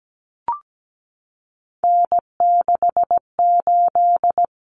Morse code, heard as a keyed tone, about 17 wpm, 690 Hz. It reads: N68
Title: Common Cut Numbers - 3-Digit 17wpm